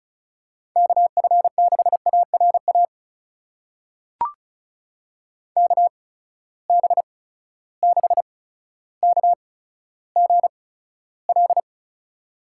Morse code, heard as a keyed tone, about 36 wpm, spaced as for 15 wpm, 700 Hz. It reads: KF6ARA KB6KGL
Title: USA Call Signs - 6 Characters - 8x Character Spacing - 35wpm